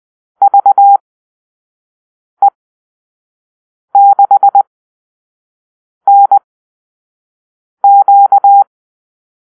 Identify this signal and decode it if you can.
Morse code, heard as a keyed tone, about 20 wpm, spaced as for 5 wpm, 800 Hz. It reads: VE6NQ
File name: arrl / Call-Sign Drill-4